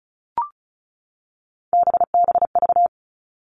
Morse code, heard as a keyed tone, about 34 wpm, 700 Hz. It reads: BBV
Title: All Cut Numbers - 3-Digit 35wpm